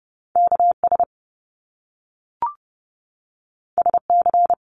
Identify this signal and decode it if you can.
Morse code, heard as a keyed tone, about 30 wpm, 700 Hz. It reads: KS SC
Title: US States 30wpm